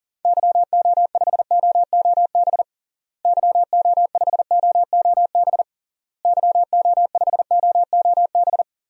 Morse code, heard as a keyed tone, about 37 wpm, 700 Hz. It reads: YO5OOB YO5OOB YO5OOB
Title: DXCC-100 Call Signs - 6 Characters - Repeated 3x - 40wpm